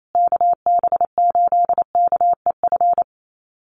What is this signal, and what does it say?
Morse code, heard as a keyed tone, about 28 wpm, 700 Hz. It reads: KB8KEF
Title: USA Call Signs Repeated 3x 28wpm